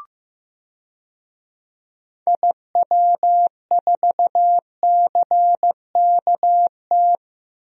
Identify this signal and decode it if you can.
Morse code, heard as a keyed tone, about 15 wpm, 700 Hz. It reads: IW4CKT
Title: DXCC-100 Call Signs - 6 Characters - 15wpm